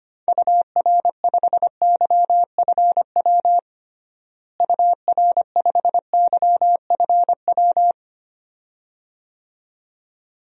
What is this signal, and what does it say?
Morse code, heard as a keyed tone, about 25 wpm, 700 Hz. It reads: UR5YFW UR5YFW
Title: DXCC-100 Call Signs - 6 Characters - Repeated 2x - 25wpm